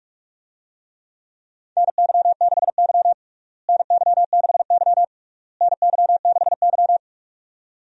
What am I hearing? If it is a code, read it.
Morse code, heard as a keyed tone, about 46 wpm, 700 Hz. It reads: NY6Y NY6Y NY6Y
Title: USA Call Signs - 4 Characters - Repeated 3x - 45wpm